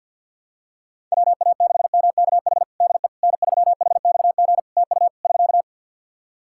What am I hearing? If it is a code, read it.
Morse code, heard as a keyed tone, about 52 wpm, 700 Hz. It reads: WA6MKH DE N3HXG TU <SK>
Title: Practice QSO - Exchange 4 - Ending - 50wpm